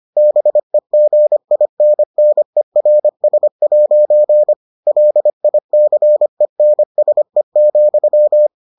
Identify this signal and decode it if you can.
Morse code, heard as a keyed tone, about 25 wpm, 600 Hz. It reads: BEGINNERS' LICENSE,